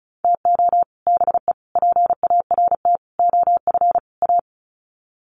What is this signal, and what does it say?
Morse code, heard as a keyed tone, about 34 wpm, 700 Hz. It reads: TO BE PART OF A
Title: Sets of 5 Words Encore2 35wpm